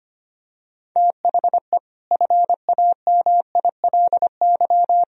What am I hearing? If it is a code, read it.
Morse code, heard as a keyed tone, about 25 wpm, 700 Hz. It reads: THE FAMILY
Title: Sentences from Top 1000 Words - Review Part 2 - 25wpm